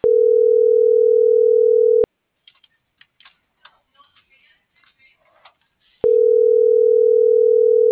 Ringback tone (North America), 440 + 480 Hz.